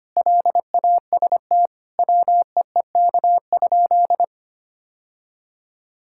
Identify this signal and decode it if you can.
Morse code, heard as a keyed tone, about 25 wpm, 700 Hz. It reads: LAST WEEK?
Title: Sentences from Top 1000 Words - Review Part 5 - 25wpm